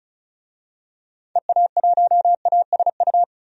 Morse code, heard as a keyed tone, about 36 wpm, 700 Hz. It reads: EA1ASU
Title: DXCC-100 Call Signs - 6 Characters - 35wpm